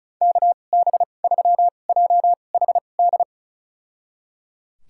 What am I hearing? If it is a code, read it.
Morse code, heard as a keyed tone, about 34 wpm, spaced as for 25 wpm, 700 Hz. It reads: KB3JHD
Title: USA Call Signs - 6 Characters - 2x Character Spacing - 35wpm